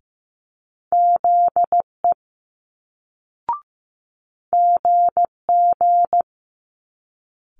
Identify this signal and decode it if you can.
Morse code, heard as a keyed tone, about 15 wpm, 700 Hz. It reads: ZE GG
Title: Multi-letter Phonemes Quick Fire 15wpm